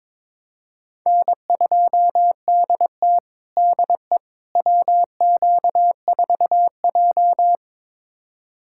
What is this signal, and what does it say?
Morse code, heard as a keyed tone, about 22 wpm, 700 Hz. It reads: N2DT DE WQ4J
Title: USA Callsign DE Callsign - 4 Characters - 22wpm